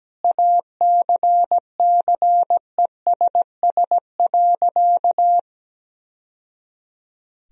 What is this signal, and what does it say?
Morse code, heard as a keyed tone, about 17 wpm, 700 Hz. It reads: ACCESS.
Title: Sentences from Top 1000 Words - Review Part 4 - 17wpm